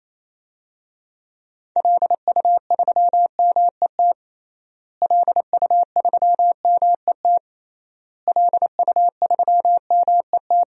Morse code, heard as a keyed tone, about 28 wpm, 700 Hz. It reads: LU3MET LU3MET LU3MET
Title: DXCC-100 Call Signs - 6 Characters - Repeated 3x - 28wpm